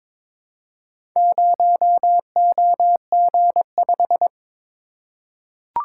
Morse code, heard as a keyed tone, about 22 wpm, 700 Hz. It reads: 0OG5
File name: practice / 22wpm / Four Letter-Number Combinations - 22wpm